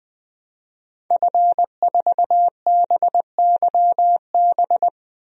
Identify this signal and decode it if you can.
Morse code, heard as a keyed tone, about 20 wpm, 700 Hz. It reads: F4BYB
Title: DXCC-100 Call Signs - 5 Characters - 20wpm